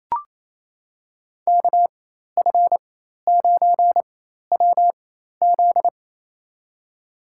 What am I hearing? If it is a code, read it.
Morse code, heard as a keyed tone, about 28 wpm, spaced as for 13 wpm, 700 Hz. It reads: KF9WZ
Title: USA Call Signs - 5 Characters - 4x Character Spacing - 28wpm